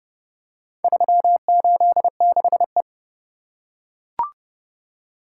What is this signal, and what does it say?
Morse code, heard as a keyed tone, about 27 wpm, 700 Hz. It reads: 386E